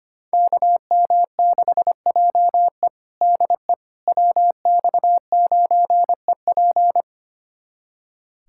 Morse code, heard as a keyed tone, about 25 wpm, 700 Hz. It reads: KM6JE DE WX9EP
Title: USA Callsign DE Callsign 25wpm